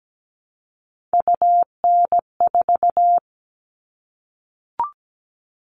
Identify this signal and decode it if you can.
Morse code, heard as a keyed tone, about 17 wpm, 700 Hz. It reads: UN4
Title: All Cut Numbers - 3-Digit 17wpm